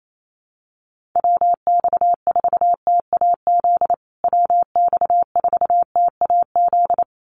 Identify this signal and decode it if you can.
Morse code, heard as a keyed tone, about 28 wpm, 700 Hz. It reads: WX4TAZ WX4TAZ
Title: USA Call Signs Repeated 2x 28wpm